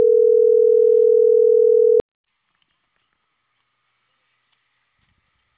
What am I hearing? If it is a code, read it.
Ringback tone (North America), 440 + 480 Hz.